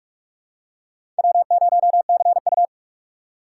Morse code, heard as a keyed tone, about 46 wpm, 700 Hz. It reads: W0CU
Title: USA Call Signs - 4 Characters - 4x Character Spacing - 45wpm